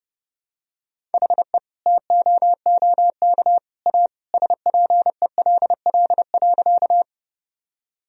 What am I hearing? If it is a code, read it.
Morse code, heard as a keyed tone, about 32 wpm, 700 Hz. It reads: HE TOOK A SPELL.
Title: Sentences from Top 1000 Words - Review Part 1 - 30wpm